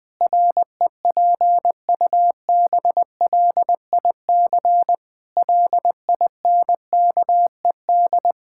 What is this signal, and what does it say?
Morse code, heard as a keyed tone, about 20 wpm, 700 Hz. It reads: REPUBLIC LINKED
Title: Pangram Challenge v2 - 20wpm